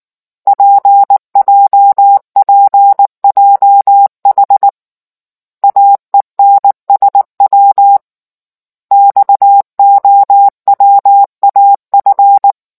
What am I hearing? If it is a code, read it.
Morse code, heard as a keyed tone, about 19 wpm, 800 Hz. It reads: PJPJH AENSW XOWAF